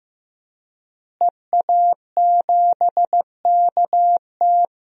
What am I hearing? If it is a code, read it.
Morse code, heard as a keyed tone, about 15 wpm, 700 Hz. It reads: EA7KT
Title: DXCC-100 Call Signs - 5 Characters - Repeated 3x - 15wpm